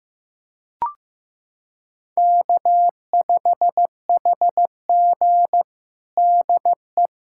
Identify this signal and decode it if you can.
Morse code, heard as a keyed tone, about 15 wpm, 700 Hz. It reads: K5HG DE
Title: USA Callsign DE Callsign - 4 Characters - 15wpm